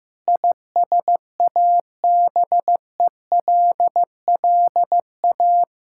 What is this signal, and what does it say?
Morse code, heard as a keyed tone, about 15 wpm, 700 Hz. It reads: ISABELLA
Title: Practice QSO - Exchange 1 - Beginning - 15wpm